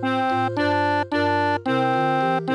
There is no code.